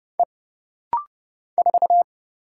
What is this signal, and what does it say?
Morse code, heard as a keyed tone, about 31 wpm, 700 Hz. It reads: E 4
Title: Single Letter-Number - Rapid-Fire 30wpm